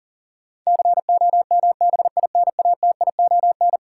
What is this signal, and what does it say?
Morse code, heard as a keyed tone, about 40 wpm, 700 Hz. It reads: COMBINATION
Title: Sentences from Top 1300 Words - New Word Drill Alt - 40wpm